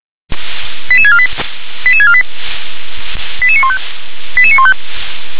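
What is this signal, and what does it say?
Five-tone selective call (ZVEI), heard as quick runs of five tones, tones of about 70 ms. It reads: 88648 88648 88915 88915
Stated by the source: ZVEI five-tone selective call 88747, received on an SDR.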